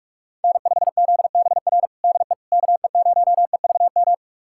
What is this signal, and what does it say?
Morse code, heard as a keyed tone, about 47 wpm, 700 Hz. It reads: N5ZBR DE KE0EVM
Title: USA Callsign DE Callsign 45wpm